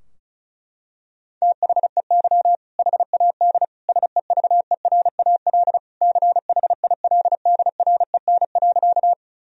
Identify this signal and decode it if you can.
Morse code, heard as a keyed tone, about 36 wpm, 700 Hz. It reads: THEY HAD SEVERAL CHILDREN.
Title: Sentences from Top 1000 Words - Review Part 2 - 35wpm